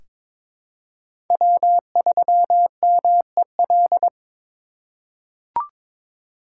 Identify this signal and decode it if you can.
Morse code, heard as a keyed tone, about 22 wpm, 700 Hz. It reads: W3MEL
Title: K1USN SST QSO Practice - 22wpm